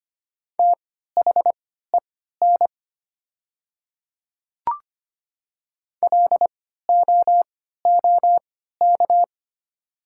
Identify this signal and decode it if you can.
Morse code, heard as a keyed tone, about 25 wpm, spaced as for 14 wpm, 700 Hz. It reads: THEN LOOK
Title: 3x Character Spacing - Top 100 Words - 25wpm